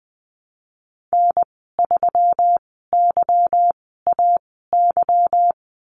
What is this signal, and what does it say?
Morse code, heard as a keyed tone, about 20 wpm, spaced as for 15 wpm, 700 Hz. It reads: N3YAY
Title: 147 Lesson 24g - Letter y - Call Signs 20wpm